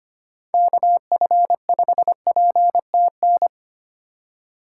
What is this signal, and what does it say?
Morse code, heard as a keyed tone, about 25 wpm, 700 Hz. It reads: KF5PTN
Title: USA Call Signs - 6 Characters - 2x Character Spacing - 25wpm